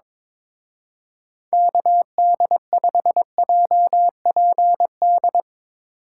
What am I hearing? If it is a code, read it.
Morse code, heard as a keyed tone, about 24 wpm, 700 Hz. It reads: KD5JPD
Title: Speed Racing - USA 2x3 Call Signs 22wpm